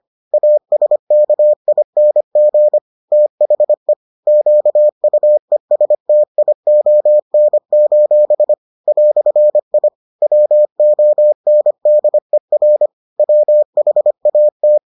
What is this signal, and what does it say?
Morse code, heard as a keyed tone, about 25 wpm, 600 Hz. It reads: ASKING THE QUESTION: "I WONDER WHAT